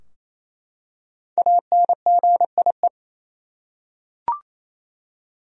Morse code, heard as a keyed tone, about 27 wpm, 700 Hz. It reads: ANGIE
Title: USA Common Names - Max 5 Letters - 28wpm